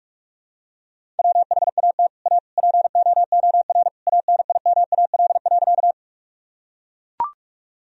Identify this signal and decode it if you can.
Morse code, heard as a keyed tone, about 41 wpm, 700 Hz. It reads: WHAT A POOR ANIMAL.
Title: Sentences from Top 1000 Words - Review Part 1 - 45wpm